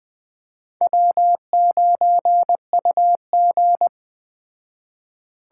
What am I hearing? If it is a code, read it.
Morse code, heard as a keyed tone, about 20 wpm, 700 Hz. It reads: W9UG
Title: Speed Racing - USA Call Signs 17wpm